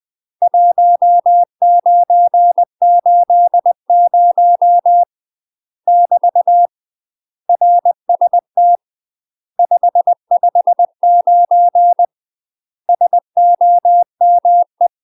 Morse code, heard as a keyed tone, about 20 wpm, 700 Hz. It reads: 1980 <BT> RST 559 SOME
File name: UK MASTER QSOs TRAINING / 20WPM / UK MASTER QSOs 0057